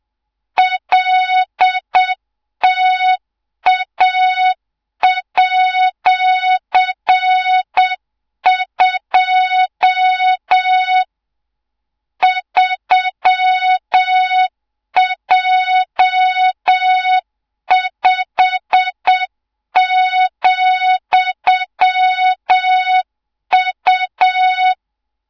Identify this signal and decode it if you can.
Morse code, heard as a keyed tone, about 7 wpm, 3700 Hz, with noise. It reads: LTA@2 3J5,U
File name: Morze-HA2MN / hang / 24 GYAKORLAT